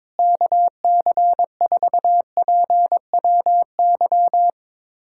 Morse code, heard as a keyed tone, about 22 wpm, 700 Hz. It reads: KC4PWY
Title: USA Callsign DE Callsign 22wpm